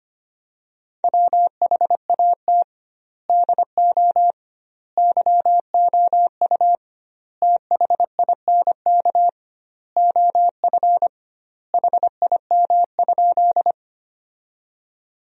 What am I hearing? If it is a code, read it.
Morse code, heard as a keyed tone, about 25 wpm, 700 Hz. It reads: WHAT DO YOU THINK OF HIM?
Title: 2x Word-Spacing - Sentences from Top 200 Words - 25wpm